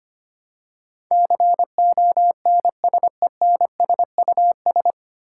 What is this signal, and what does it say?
Morse code, heard as a keyed tone, about 25 wpm, 700 Hz. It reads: CONSENSUS
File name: Nine Letter Words - 25wpm